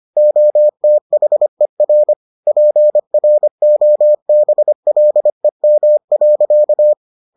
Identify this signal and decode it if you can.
Morse code, heard as a keyed tone, about 25 wpm, 600 Hz. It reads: OTHER PROBLEM.